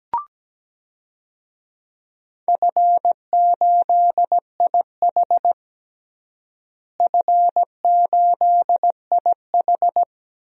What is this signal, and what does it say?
Morse code, heard as a keyed tone, about 17 wpm, 700 Hz. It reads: F8IH F8IH
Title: DXCC-100 Call Signs - 4 Characters - Repeated 3x - 17wpm